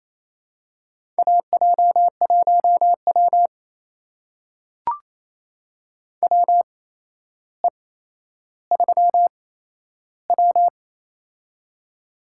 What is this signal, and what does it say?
Morse code, heard as a keyed tone, about 28 wpm, spaced as for 12 wpm, 700 Hz. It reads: AJ1W WE3W
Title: USA Call Signs - 4 Characters - 8x Character Spacing - 28wpm